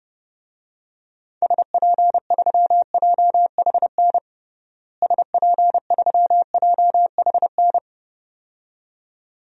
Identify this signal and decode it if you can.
Morse code, heard as a keyed tone, about 30 wpm, 700 Hz. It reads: SP3JHN SP3JHN
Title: DXCC-100 Call Signs - 6 Characters - Repeated 2x - 30wpm